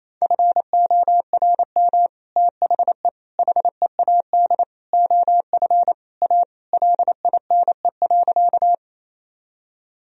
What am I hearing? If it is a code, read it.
Morse code, heard as a keyed tone, about 28 wpm, 700 Hz. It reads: FORM THE HEAD OF A LINE.
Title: Sentences from Top 1000 Words - Review Part 1 - 28wpm